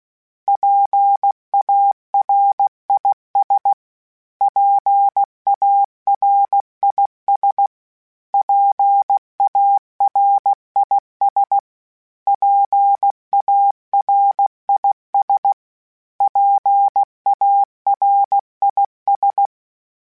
Morse code, heard as a keyed tone, about 16 wpm, 800 Hz. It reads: PARIS PARIS PARIS PARIS PARIS
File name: PARIS Calibration / PARIS mp3s / PARIS@5-@15